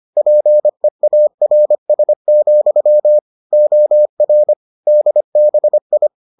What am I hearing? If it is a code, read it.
Morse code, heard as a keyed tone, about 25 wpm, 600 Hz. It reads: PEARS, OR DBI